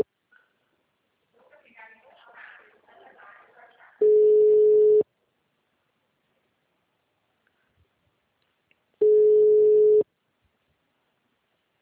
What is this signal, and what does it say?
Ringback tone (Europe, 425 Hz), with speech.
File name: out- -2005- - - .36